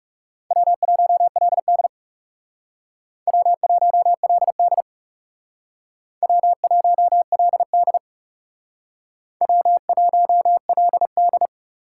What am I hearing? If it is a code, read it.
Morse code, heard as a keyed tone, about 36 wpm, 700 Hz. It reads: W1LD W1LD W1LD W1LD
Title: Speed Racing - USA Call Signs 30wpm